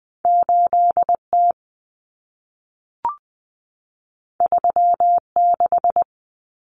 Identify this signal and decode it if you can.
Morse code, heard as a keyed tone, about 20 wpm, 700 Hz. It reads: 8T 36
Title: Common Cut Numbers - 2-Digit 20wpm